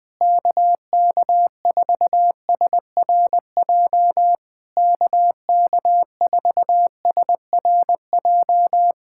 Morse code, heard as a keyed tone, about 20 wpm, 700 Hz. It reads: KK4SRJ KK4SRJ
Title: Speed-Racing Common QSO Phrases - 20wpm